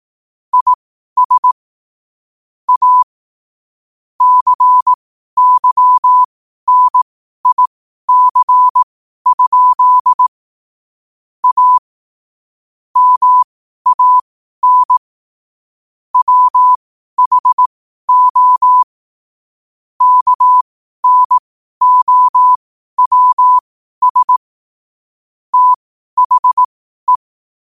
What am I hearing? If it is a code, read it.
Morse code, heard as a keyed tone, about 18 wpm, spaced as for 12 wpm, 1000 Hz. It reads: IS A CYNIC? A MAN WHO KNOWS THE
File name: qotdfiles / 12wpm